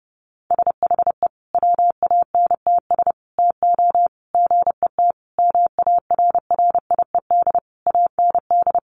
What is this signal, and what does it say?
Morse code, heard as a keyed tone, about 30 wpm, 700 Hz. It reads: SHE WANTS TO GET MARRIED AND
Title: Easy Phrases 30wpm